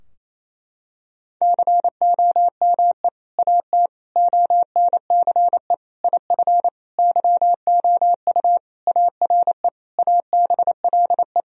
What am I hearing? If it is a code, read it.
Morse code, heard as a keyed tone, about 28 wpm, 700 Hz. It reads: COME AT ONCE IF YOU ARE ABLE
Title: Sentences from Top 1000 Words - Review Part 2 - 28wpm